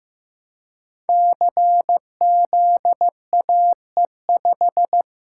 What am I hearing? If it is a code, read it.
Morse code, heard as a keyed tone, about 15 wpm, 700 Hz. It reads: CZAE5
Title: Code Groups - 5 Letter:Numbers - 15wpm